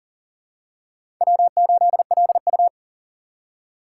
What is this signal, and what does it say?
Morse code, heard as a keyed tone, about 40 wpm, 700 Hz. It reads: W8LU
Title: USA Call Signs - 6x Character Spacing - 40wpm